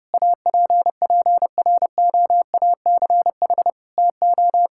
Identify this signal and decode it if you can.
Morse code, heard as a keyed tone, about 30 wpm, 700 Hz. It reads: APPROACH TO
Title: Sentences from Top 1000 Words - Review Part 6 - 30wpm